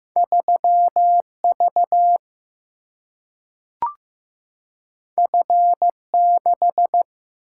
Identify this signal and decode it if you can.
Morse code, heard as a keyed tone, about 15 wpm, 700 Hz. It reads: 3 V F 6
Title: Single-Letter Single-Number 15wpm